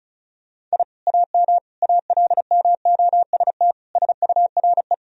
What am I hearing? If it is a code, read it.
Morse code, heard as a keyed tone, about 34 wpm, 700 Hz. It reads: I AM ALMOST SURE
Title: Sentences from Top 1000 Words - Review Part 4 - 35wpm